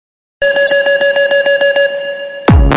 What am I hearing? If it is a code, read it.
There is no code.